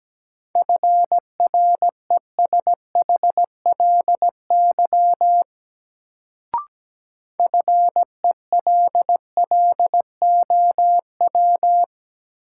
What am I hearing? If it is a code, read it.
Morse code, heard as a keyed tone, about 17 wpm, 700 Hz. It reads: FRESHLY FELLOW
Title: Sets of 5 Words Encore - New Word Drill - 17wpm